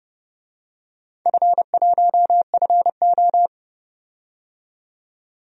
Morse code, heard as a keyed tone, about 30 wpm, 700 Hz. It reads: F1FO
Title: DXCC-100 Call Signs - 4 Characters - 30wpm